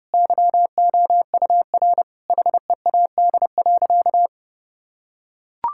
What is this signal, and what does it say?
Morse code, heard as a keyed tone, about 32 wpm, 700 Hz. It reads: YOUR HEAD.
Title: Sentences from Top 1600 Words 30wpm